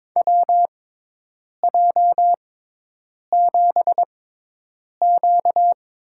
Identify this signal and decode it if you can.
Morse code, heard as a keyed tone, about 22 wpm, spaced as for 8 wpm, 700 Hz. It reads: WJ7Q
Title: USA Call Signs - 4 Characters - 6x Character Spacing - 22wpm